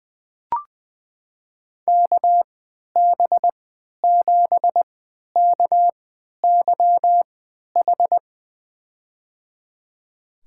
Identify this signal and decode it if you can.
Morse code, heard as a keyed tone, about 20 wpm, spaced as for 11 wpm, 700 Hz. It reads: KB7KYH
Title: USA Call Signs - 3x Character Spacing - 20wpm